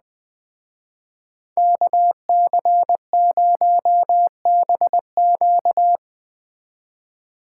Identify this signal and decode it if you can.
Morse code, heard as a keyed tone, about 20 wpm, 700 Hz. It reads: KC0BQ